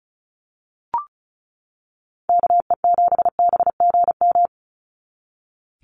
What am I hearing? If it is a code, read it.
Morse code, heard as a keyed tone, about 34 wpm, 700 Hz. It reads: KE7BGM
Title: NA SOTA Activator Callsigns 35wpm